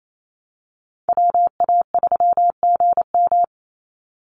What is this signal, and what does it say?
Morse code, heard as a keyed tone, about 28 wpm, 700 Hz. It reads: WA3GM
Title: Call Signs - 28wpm